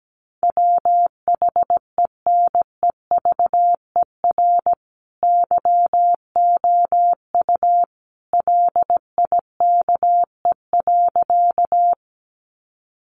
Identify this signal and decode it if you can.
Morse code, heard as a keyed tone, about 17 wpm, 700 Hz. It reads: WHENEVER YOU LIKE.
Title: Sentences from Top 1200 Words 17wpm